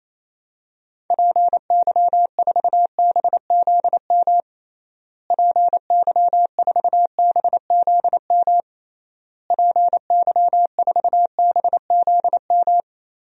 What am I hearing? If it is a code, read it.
Morse code, heard as a keyed tone, about 28 wpm, 700 Hz. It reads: PY4BZM PY4BZM PY4BZM
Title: DXCC-100 Call Signs - 6 Characters - Repeated 3x - 28wpm